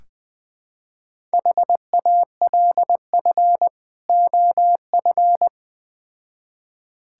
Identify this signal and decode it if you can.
Morse code, heard as a keyed tone, about 20 wpm, 700 Hz. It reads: HALF OF
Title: Build A Phrase - 3-Word Phrases - 20wpm